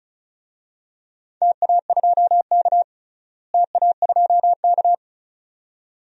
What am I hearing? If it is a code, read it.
Morse code, heard as a keyed tone, about 36 wpm, 700 Hz. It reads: TA2K TA2K
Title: DXCC-100 Call Signs - 4 Characters - Repeated 2x - 35wpm